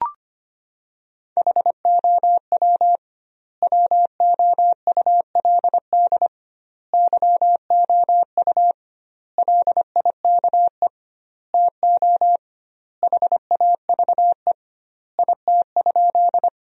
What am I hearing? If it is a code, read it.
Morse code, heard as a keyed tone, about 25 wpm, 700 Hz. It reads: HOW WOULD YOU LIKE TO HAVE IT?
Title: 2x Word-Spacing - Sentences from Top 100 Words - 25wpm